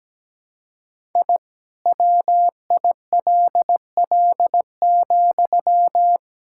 Morse code, heard as a keyed tone, about 17 wpm, 700 Hz. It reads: I WILL,
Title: Sentences from Top 1000 Words - Review Part 5 - 17wpm